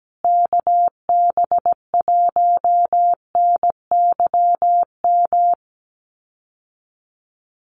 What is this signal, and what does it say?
Morse code, heard as a keyed tone, about 17 wpm, 700 Hz. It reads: KB1NYM
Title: USA Call Signs Repeated 2x 17wpm